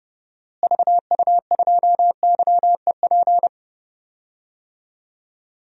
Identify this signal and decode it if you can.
Morse code, heard as a keyed tone, about 32 wpm, 700 Hz. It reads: VU2YEP
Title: DXCC-100 Call Signs - 6 Characters - Repeated 2x - 30wpm